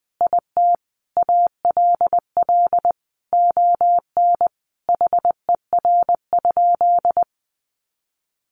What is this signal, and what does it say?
Morse code, heard as a keyed tone, about 20 wpm, 700 Hz. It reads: IT ALL ON HER?
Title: Sentences from Top 800 Words 20wpm